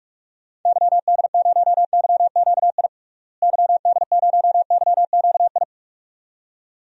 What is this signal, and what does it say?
Morse code, heard as a keyed tone, about 46 wpm, 700 Hz. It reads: YD0YQI YD0YQI
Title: DXCC-100 Call Signs - 6 Characters - Repeated 2x - 45wpm